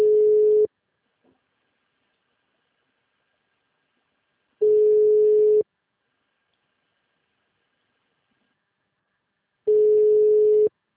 Ringback tone (Europe, 425 Hz).
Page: Duque de Caxias